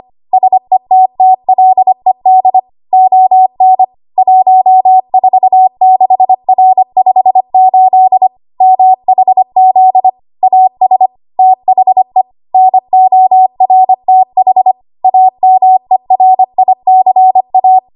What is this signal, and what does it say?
Morse code, heard as a keyed tone, about 25 wpm, 750 Hz. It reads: SETTLED ON 146R58 MHZ AS THE NORTH AMERICA
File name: Morse / 25WPM